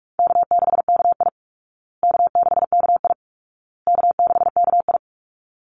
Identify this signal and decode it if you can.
Morse code, heard as a keyed tone, about 44 wpm, 700 Hz. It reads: K6KI K6KI K6KI
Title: Call Signs Repeated 2x - 45wpm